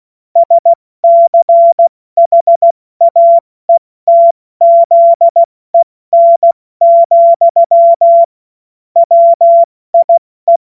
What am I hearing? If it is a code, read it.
Morse code, heard as a keyed tone, about 16 wpm, spaced as for 14 wpm, 680 Hz. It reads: SCHAETZEN, WIE
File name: Morsekurs / Witze / Witz 13. September 2022 70 F80